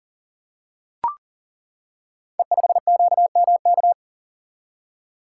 Morse code, heard as a keyed tone, about 39 wpm, 700 Hz. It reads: E5QMK
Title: Code Groups - 5 Letter:Numbers - 40wpm